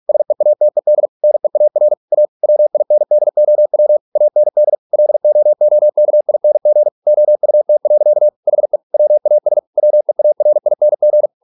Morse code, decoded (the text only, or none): SEATED NEAR A WINDOW AND LOOKING OUT. HE WAS WEARING